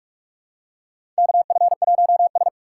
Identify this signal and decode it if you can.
Morse code, heard as a keyed tone, about 46 wpm, 700 Hz. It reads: KF1S
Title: USA Call Signs - 4 Characters - 4x Character Spacing - 45wpm